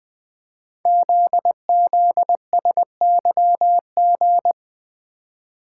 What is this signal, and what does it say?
Morse code, heard as a keyed tone, about 20 wpm, 700 Hz. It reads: ZZSYG
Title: Code Groups - 5 Letters - 20wpm